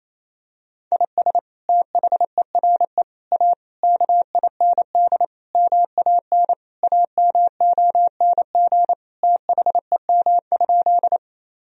Morse code, heard as a keyed tone, about 28 wpm, 700 Hz. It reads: IS THERE A KIND MAN AMONG THEM?
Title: Sentences from Top 1000 Words - Review Part 3 - 28wpm